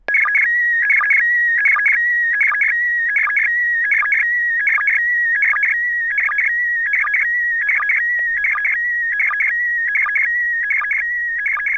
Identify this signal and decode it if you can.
SSTV (slow-scan television), PD 180.
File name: iss-sstv